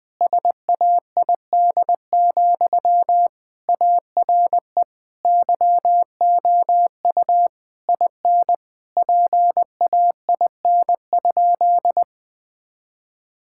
Morse code, heard as a keyed tone, about 20 wpm, 700 Hz. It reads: SAID, ARE YOU IN PAIN?
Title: Sentences from Top 1600 Words 20wpm